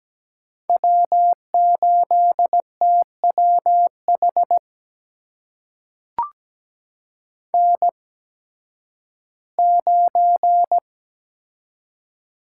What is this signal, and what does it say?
Morse code, heard as a keyed tone, about 17 wpm, spaced as for 11 wpm, 700 Hz. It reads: W8TWH N9
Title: USA Call Signs - 5 Characters - 8x Character Spacing - 17wpm